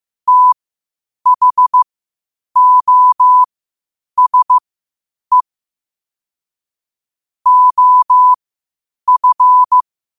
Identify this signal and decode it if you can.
Morse code, heard as a keyed tone, about 15 wpm, spaced as for 8 wpm, 1000 Hz. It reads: THOSE OF
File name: qotdfiles / 8wpm